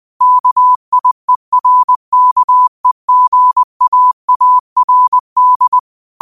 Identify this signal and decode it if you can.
Morse code, heard as a keyed tone, about 20 wpm, 1000 Hz. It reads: KIERKEGAARD
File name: qotdfiles / 20wpm